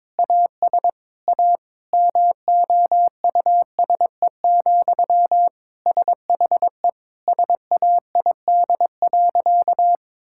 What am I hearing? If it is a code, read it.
Morse code, heard as a keyed tone, about 22 wpm, 700 Hz. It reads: AS A MOUSE, SHE SAID.
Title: Sentences from Top 1000 Words - Review Part 5 - 22wpm